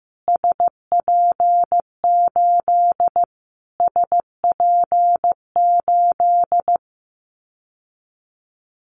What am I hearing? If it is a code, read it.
Morse code, heard as a keyed tone, about 15 wpm, 700 Hz. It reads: SP8 SP8
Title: Callsign Prefixes for Top 25 DXCC - Repeated 3x - 15wpm